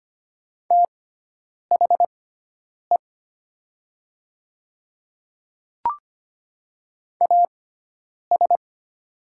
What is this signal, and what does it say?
Morse code, heard as a keyed tone, about 25 wpm, spaced as for 9 wpm, 700 Hz. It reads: THE AS
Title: 6x Character Spacing - Top 100 Words - 25wpm